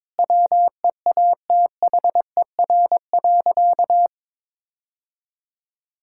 Morse code, heard as a keyed tone, about 22 wpm, 700 Hz. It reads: WEATHER.
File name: Sentences with 9-Word Phrases - 22wpm